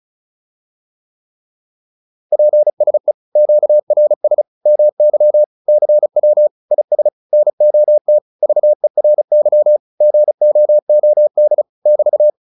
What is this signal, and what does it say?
Morse code, heard as a keyed tone, about 36 wpm, 600 Hz. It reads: PSE QRS MY CW IS NOT VERY GOOD <BT>